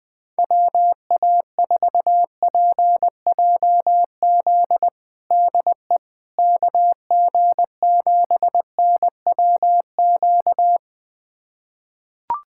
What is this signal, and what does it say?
Morse code, heard as a keyed tone, about 21 wpm, 700 Hz. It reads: WA4PJZ DE KG7NWQ
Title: USA Callsign DE Callsign 20wpm